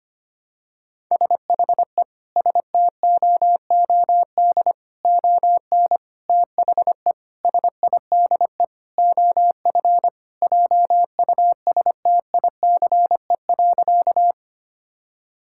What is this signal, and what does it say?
Morse code, heard as a keyed tone, about 25 wpm, 700 Hz. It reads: SHE STOOD ON THE SIDE OF JUSTICE.
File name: practice / 25wpm / Sentences with 4-Word Phrases - 25wpm